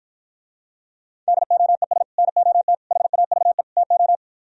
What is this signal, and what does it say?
Morse code, heard as a keyed tone, about 53 wpm, 700 Hz. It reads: DOES NOT HAVE TO